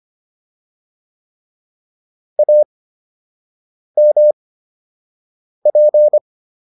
Morse code, heard as a keyed tone, about 25 wpm, spaced as for 6 wpm, 600 Hz. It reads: AMP